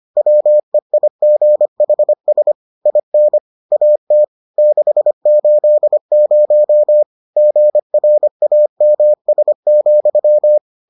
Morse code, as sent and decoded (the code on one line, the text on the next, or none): .-- . .. --. .... ... / .. -. / .- - / -.... ---.. ----- / --. .-. .- -- ... --..--
WEIGHS IN AT 680 GRAMS,